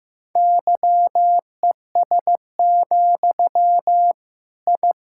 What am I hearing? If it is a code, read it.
Morse code, heard as a keyed tone, about 16 wpm, 700 Hz. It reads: YES, I
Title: Sentences from Top 1000 Words - Review Part 6 - 15wpm